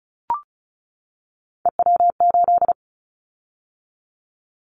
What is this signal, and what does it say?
Morse code, heard as a keyed tone, about 34 wpm, 700 Hz. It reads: EW8
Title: Callsign Prefixes for Top 100 DXCC - 35wpm